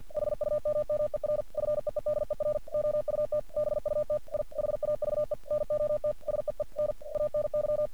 Morse code, heard as a keyed tone, about 46 wpm, 600 Hz. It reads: HUMMER FEEDER OUT BUT I HAVE NOT SEEN ANY
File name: archived / CRSnet